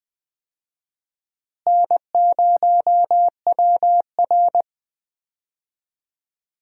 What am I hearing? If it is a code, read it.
Morse code, heard as a keyed tone, about 20 wpm, 700 Hz. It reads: N0WR
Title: USA Call Signs - 4 Characters - Repeated 2x - 20wpm